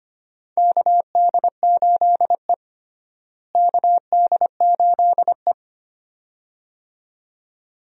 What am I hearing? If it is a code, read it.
Morse code, heard as a keyed tone, about 25 wpm, 700 Hz. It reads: KD8E KD8E
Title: USA Call Signs - 4 Characters - Repeated 2x - 25wpm